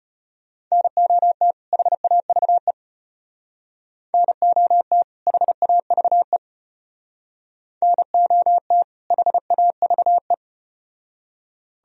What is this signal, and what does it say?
Morse code, heard as a keyed tone, about 34 wpm, 700 Hz. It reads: NOT HAVE NOT HAVE NOT HAVE